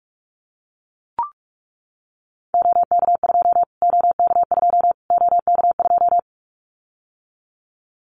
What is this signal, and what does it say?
Morse code, heard as a keyed tone, about 44 wpm, 700 Hz. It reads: OK2 OK2 OK2
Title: Callsign Prefixes for Top 25 DXCC - Repeated 3x - 45wpm